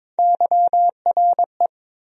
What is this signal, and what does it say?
Morse code, heard as a keyed tone, about 22 wpm, 700 Hz. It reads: YRE
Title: USA Call Sign Postfixes - 3 Letters 22wpm